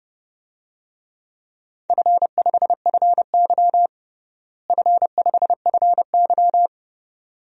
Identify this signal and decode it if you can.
Morse code, heard as a keyed tone, about 32 wpm, 700 Hz. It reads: F5FY F5FY
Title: DXCC-100 Call Signs - 4 Characters - Repeated 2x - 30wpm